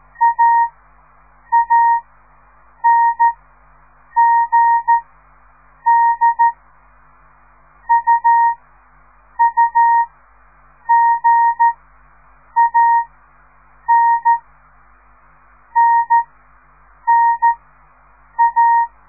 Morse code, heard as a keyed tone, about 14 wpm, 930 Hz. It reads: A A N G D U U G A N N N A
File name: radiomorningstar / morse / les 4